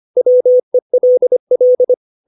Morse code, heard as a keyed tone, about 25 wpm, 500 Hz. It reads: WELL